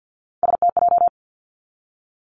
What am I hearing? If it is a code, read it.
Morse code, heard as a keyed tone, about 50 wpm, 700 Hz. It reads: STJ
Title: Three Challenging Letter-Number Combinations 50wpm